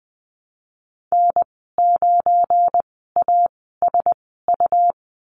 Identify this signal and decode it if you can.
Morse code, heard as a keyed tone, about 20 wpm, spaced as for 15 wpm, 700 Hz. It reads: N9ASU
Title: 179 Lesson 29g - Slash - Call Signs 20wpm